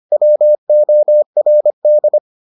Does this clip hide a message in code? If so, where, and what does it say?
Yes: Morse code from 0.1 to 2.2 s, heard as a keyed tone, about 25 wpm, 600 Hz. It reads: WORD